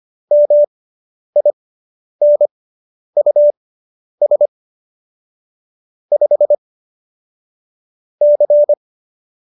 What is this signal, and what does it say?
Morse code, heard as a keyed tone, about 25 wpm, spaced as for 10 wpm, 590 Hz. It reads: MINUS 5 C